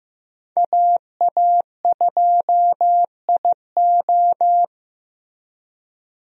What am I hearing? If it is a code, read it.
Morse code, heard as a keyed tone, about 16 wpm, 700 Hz. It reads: AA2IO
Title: USA Call Signs - 5 Characters - Repeated 3x - 15wpm